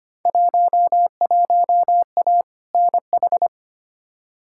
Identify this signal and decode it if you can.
Morse code, heard as a keyed tone, about 25 wpm, 700 Hz. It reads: 11A NH
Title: Field Day Exchanges Encore - 25wpm